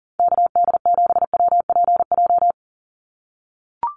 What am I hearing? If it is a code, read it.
Morse code, heard as a keyed tone, about 40 wpm, 700 Hz. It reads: KD7WPJ
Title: NA SOTA Activator Callsigns 40wpm